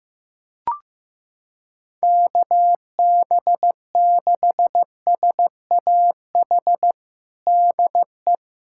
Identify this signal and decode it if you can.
Morse code, heard as a keyed tone, about 15 wpm, 700 Hz. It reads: KB6SAH DE
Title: USA Callsign DE Callsign 15wpm